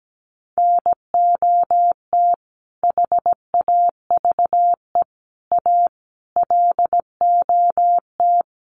Morse code, heard as a keyed tone, about 17 wpm, 700 Hz. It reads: NOT HAVE A LOT
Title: Sets of 5 Words 17wpm